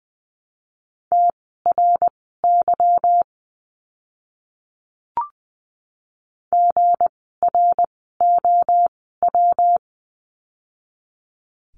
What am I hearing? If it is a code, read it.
Morse code, heard as a keyed tone, about 20 wpm, spaced as for 15 wpm, 700 Hz. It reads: TRY GROW
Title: 160 Lesson 26f - Letter g - Words 20wpm